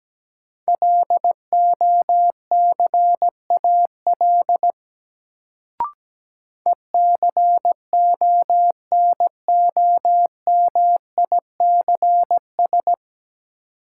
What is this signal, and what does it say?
Morse code, heard as a keyed tone, about 17 wpm, 700 Hz. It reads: LOCAL ECONOMICS
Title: Sentences from Top 1000 Words - Review Part 3 - New Word Drill - 17wpm